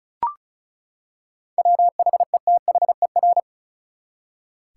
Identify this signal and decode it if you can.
Morse code, heard as a keyed tone, about 36 wpm, 700 Hz. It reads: WHETHER
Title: Seven Letter Words - 35wpm